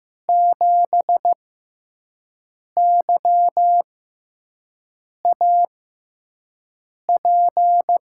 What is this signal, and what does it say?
Morse code, heard as a keyed tone, about 15 wpm, spaced as for 5 wpm, 700 Hz. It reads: 7YAP
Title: USA Call Signs - 6x Character Spacing - 15wpm